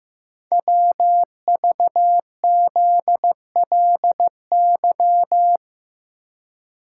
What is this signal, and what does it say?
Morse code, heard as a keyed tone, about 15 wpm, 690 Hz. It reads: WVZLY
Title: Code Groups - 5 Letters - 15wpm